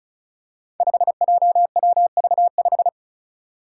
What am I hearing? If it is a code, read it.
Morse code, heard as a keyed tone, about 34 wpm, 700 Hz. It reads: 5JWV5
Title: Five Challenging Letter-Number Combinations 35wpm